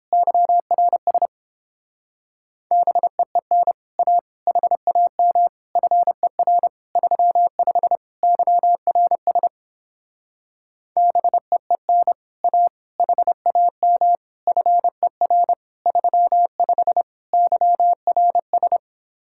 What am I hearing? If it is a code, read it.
Morse code, heard as a keyed tone, about 29 wpm, 700 Hz. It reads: YRS BEEN A HAM FER 35 YRS BEEN A HAM FER 35 YRS